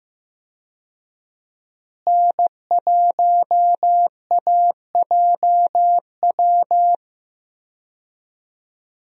Morse code, heard as a keyed tone, about 15 wpm, 700 Hz. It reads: N1AJW
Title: USA Call Signs - 5 Characters - Repeated 2x - 15wpm